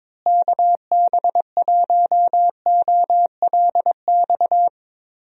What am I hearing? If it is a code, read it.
Morse code, heard as a keyed tone, about 22 wpm, 700 Hz. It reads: KB1OLX
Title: USA Call Signs - 6 Characters - Repeated 2x - 22wpm